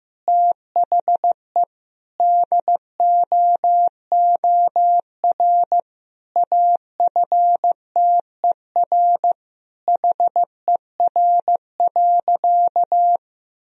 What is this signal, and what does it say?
Morse code, heard as a keyed tone, about 15 wpm, 700 Hz. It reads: THE DOOR AFTER HER.
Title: Sentences from Top 1000 Words - Review Part 1 - 15wpm